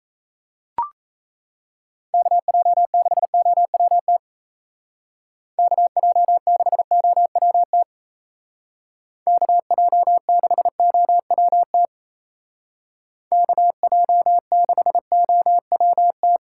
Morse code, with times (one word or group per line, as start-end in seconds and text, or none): 2.1-4.2 KJ6OWT
5.6-7.8 KJ6OWT
9.3-11.9 KJ6OWT
13.3-16.4 KJ6OWT